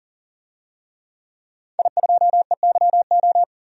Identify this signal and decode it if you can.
Morse code, heard as a keyed tone, about 40 wpm, 700 Hz. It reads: I2EYO
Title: DXCC-100 Call Signs - 5 Characters - 40wpm